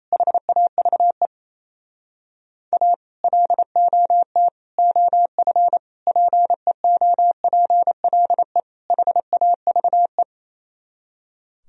Morse code, heard as a keyed tone, about 32 wpm, 700 Hz. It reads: HAVE A LOT OF PEOPLE HAVE
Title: Speed-Racing - Sets of 5 Words 28wpm